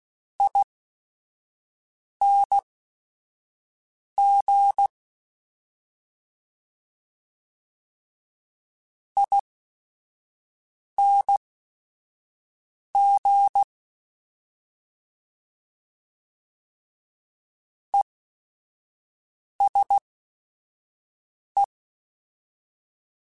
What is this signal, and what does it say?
Morse code, heard as a keyed tone, about 16 wpm, spaced as for 5 wpm, 800 Hz. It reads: ING ING ESE
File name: Short Runs / Short Runs-mp3s / ESE ING @5